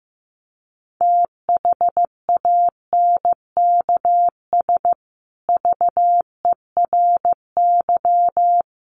Morse code, heard as a keyed tone, about 15 wpm, 700 Hz. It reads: THANKS VERY
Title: Sets of 5 Words Encore2 15wpm